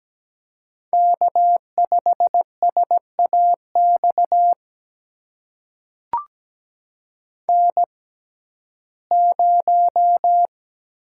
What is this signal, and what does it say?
Morse code, heard as a keyed tone, about 17 wpm, spaced as for 12 wpm, 700 Hz. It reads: K5SAX N0
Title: USA Call Signs - 5 Characters - 6x Character Spacing - 17wpm